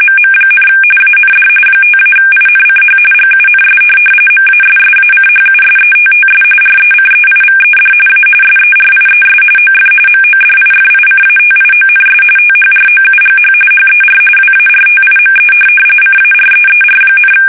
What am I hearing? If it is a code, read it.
RTTY, about 45.45 baud. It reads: MM, AND 5-INCH GUNS MASSACHUSETTS RECEIVED A SERIES OF MODIFICATIONS THROUGH HER WARTIME CAREER,
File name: fsk850 / USS-Massachusetts